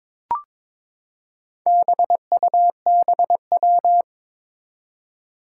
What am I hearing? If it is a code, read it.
Morse code, heard as a keyed tone, about 22 wpm, 700 Hz. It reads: BUBW
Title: Four Challenging Letter-Number Combinations 22wpm